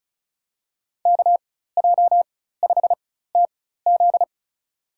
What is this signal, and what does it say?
Morse code, heard as a keyed tone, about 36 wpm, spaced as for 16 wpm, 700 Hz. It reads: KJ5TZ
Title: USA Call Signs - 5 Characters - 4x Character Spacing - 35wpm